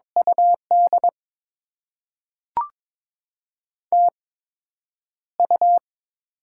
Morse code, heard as a keyed tone, about 22 wpm, 700 Hz. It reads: U D T U
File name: Top 20 QSO Elements - 8x Character Spacing - 22wpm